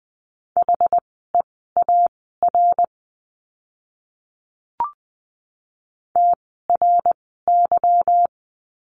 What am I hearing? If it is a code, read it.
Morse code, heard as a keyed tone, about 20 wpm, spaced as for 14 wpm, 700 Hz. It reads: HEAR TRY
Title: 206 Lesson 33f - Letter j - Words 20wpm